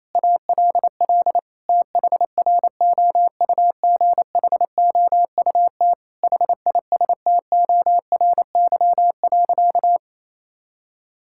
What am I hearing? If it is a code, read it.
Morse code, heard as a keyed tone, about 28 wpm, 700 Hz. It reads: ALL THROUGHOUT HISTORY.
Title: Sentences from Top 1000 Words - Review Part 5 - 28wpm